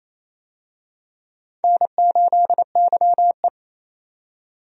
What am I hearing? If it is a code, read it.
Morse code, heard as a keyed tone, about 28 wpm, 700 Hz. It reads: N8YE
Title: USA Call Signs - 4 Characters - 28wpm